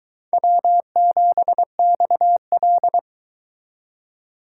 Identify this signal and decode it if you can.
Morse code, heard as a keyed tone, about 22 wpm, 700 Hz. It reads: W7XL